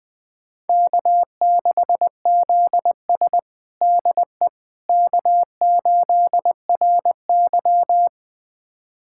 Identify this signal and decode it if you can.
Morse code, heard as a keyed tone, about 20 wpm, 700 Hz. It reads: K6ZS DE K8RY
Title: USA Callsign DE Callsign - 4 Characters - 20wpm